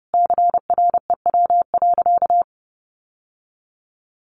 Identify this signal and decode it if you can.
Morse code, heard as a keyed tone, about 30 wpm, 700 Hz. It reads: CREW.
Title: Sentences from Top 1500 Words 30wpm